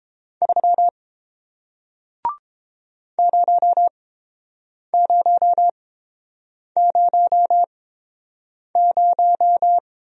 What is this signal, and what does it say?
Morse code, heard as a keyed tone, about 30 wpm, spaced as for 7 wpm, 700 Hz. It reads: 3 0000